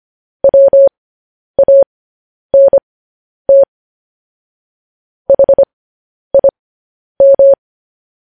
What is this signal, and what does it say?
Morse code, heard as a keyed tone, about 25 wpm, spaced as for 10 wpm, 550 Hz. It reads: WANT HIM